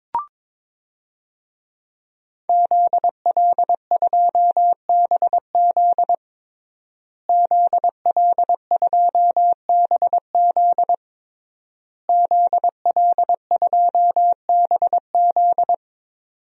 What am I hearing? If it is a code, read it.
Morse code, heard as a keyed tone, about 21 wpm, 700 Hz. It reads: ZL2BZ ZL2BZ ZL2BZ
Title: DXCC-100 Call Signs - 5 Characters - Repeated 3x - 22wpm